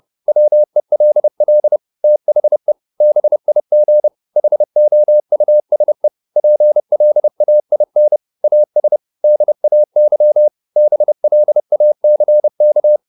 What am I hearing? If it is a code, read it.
Morse code, heard as a keyed tone, about 30 wpm, 600 Hz. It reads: WELL THE BIG HOUSE PLAIN AS DAY BLACK